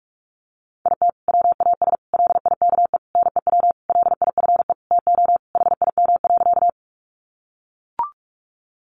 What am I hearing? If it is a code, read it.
Morse code, heard as a keyed tone, about 44 wpm, 700 Hz. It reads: IT WAS LIKE NEW LIFE TO HIM.
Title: Sentences from Top 300 Words 45wpm